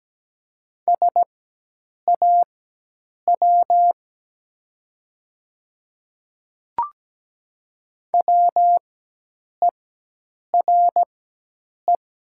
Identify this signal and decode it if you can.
Morse code, heard as a keyed tone, about 17 wpm, spaced as for 8 wpm, 700 Hz. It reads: SAW WERE
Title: 4x Character Spacing - Top 300 Words - 17wpm